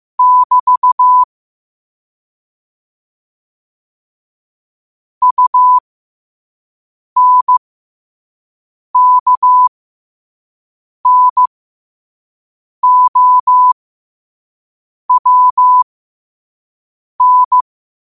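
Morse code, heard as a keyed tone, about 15 wpm, spaced as for 5 wpm, 1000 Hz. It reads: <BT> UNKNOWN
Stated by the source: Morse code at 5 wpm.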